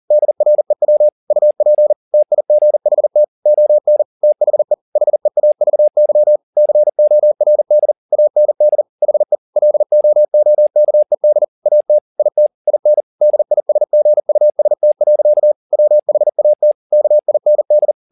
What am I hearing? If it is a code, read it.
Morse code, heard as a keyed tone, about 40 wpm, 600 Hz. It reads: DREW UP TIGHT ON THE HEAVY CORD AND HE LOOKED AT IT IN DISGUST. WHAT KIND